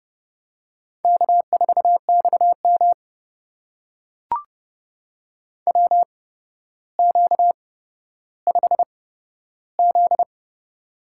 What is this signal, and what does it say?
Morse code, heard as a keyed tone, about 30 wpm, spaced as for 13 wpm, 700 Hz. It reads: K4XM WQ5Z
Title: USA Call Signs - 4 Characters - 8x Character Spacing - 30wpm